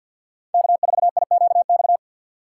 Morse code, heard as a keyed tone, about 51 wpm, 700 Hz. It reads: K4IQX
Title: USA Call Signs - 6x Character Spacing - 50wpm